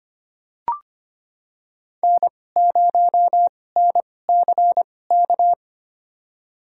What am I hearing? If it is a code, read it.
Morse code, heard as a keyed tone, about 25 wpm, spaced as for 18 wpm, 700 Hz. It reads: N0NCK
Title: USA Call Signs - 5 Characters - 2x Character Spacing - 25wpm